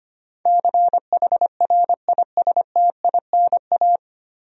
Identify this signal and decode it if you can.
Morse code, heard as a keyed tone, about 25 wpm, 700 Hz. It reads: CHRISTINA
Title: USA Common Names - 25wpm